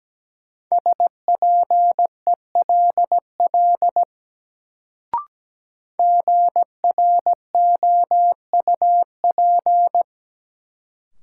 Morse code, heard as a keyed tone, about 17 wpm, 700 Hz. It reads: SPELL GROUP
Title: Sentences from Top 1000 Words - Review Part 1 - New Word Drill - 17wpm